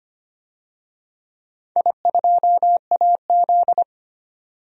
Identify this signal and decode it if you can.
Morse code, heard as a keyed tone, about 25 wpm, 700 Hz. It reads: I2AZ
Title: DXCC-100 Call Signs - 4 Characters - 25wpm